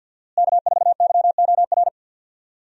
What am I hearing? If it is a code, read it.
Morse code, heard as a keyed tone, about 53 wpm, 700 Hz. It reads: K4YOR
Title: USA Call Signs - 5 Characters - Repeated 2x - 50wpm